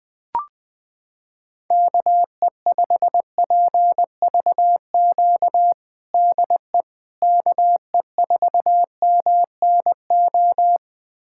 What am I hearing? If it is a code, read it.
Morse code, heard as a keyed tone, about 20 wpm, 700 Hz. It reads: KE5PVQ DE KE4MNO
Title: USA Callsign DE Callsign 20wpm